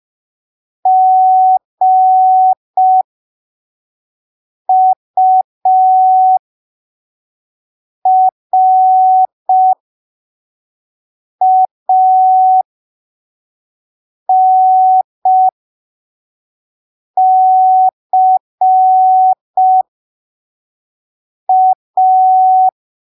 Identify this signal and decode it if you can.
Morse code, heard as a keyed tone, about 5 wpm, spaced as for 3 wpm, 750 Hz. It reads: GURANCA
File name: arquivos / texto / texto 03 5ppm